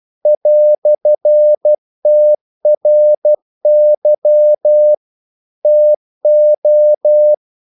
Morse code, heard as a keyed tone, about 12 wpm, 600 Hz. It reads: "TRY TO
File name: CW / Practice / EnglishQuotations0112